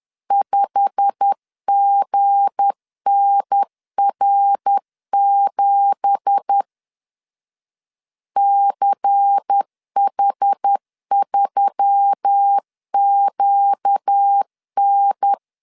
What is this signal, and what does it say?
Morse code, heard as a keyed tone, about 10 wpm, 800 Hz. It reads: 5GNR7 CH3QN
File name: CW / DET / 1 9 1500C S A 10wpm 4511